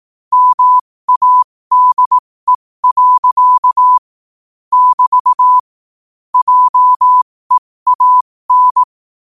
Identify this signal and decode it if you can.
Morse code, heard as a keyed tone, about 18 wpm, spaced as for 15 wpm, 1000 Hz. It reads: MADE. <BT> JEAN
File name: qotdfiles / 15wpm